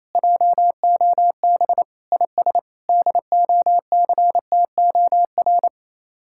Morse code, heard as a keyed tone, about 37 wpm, 700 Hz. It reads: JOB IS DOCTOR